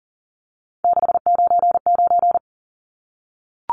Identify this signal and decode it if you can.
Morse code, heard as a keyed tone, about 40 wpm, 700 Hz. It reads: 699
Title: Numbers - 3-Digit 40wpm